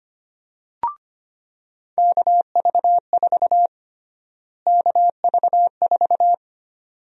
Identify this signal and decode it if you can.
Morse code, heard as a keyed tone, about 25 wpm, 700 Hz. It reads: KV4 KV4
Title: USA Call Sign Prefixes - Repeated 2x - 25wpm